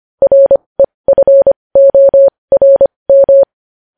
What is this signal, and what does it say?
Morse code, heard as a keyed tone, about 25 wpm, spaced as for 20 wpm, 550 Hz. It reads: REFORM
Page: News headlines in morse code at 20 wpm